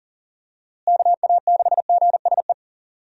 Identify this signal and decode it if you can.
Morse code, heard as a keyed tone, about 40 wpm, 700 Hz. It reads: KA6GSE
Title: USA Call Signs - 6 Characters - Repeated 3x - 40wpm